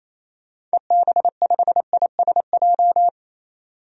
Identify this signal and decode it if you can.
Morse code, heard as a keyed tone, about 28 wpm, 700 Hz. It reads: EB5ISJ
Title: DXCC-100 Call Signs - 6 Characters - 28wpm